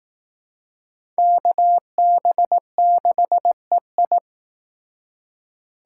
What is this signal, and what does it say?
Morse code, heard as a keyed tone, about 19 wpm, 700 Hz. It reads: KB6EI